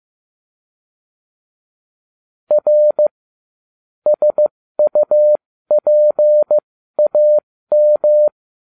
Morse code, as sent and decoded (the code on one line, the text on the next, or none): .-. / ... ..- .--. .- --
R SUPAM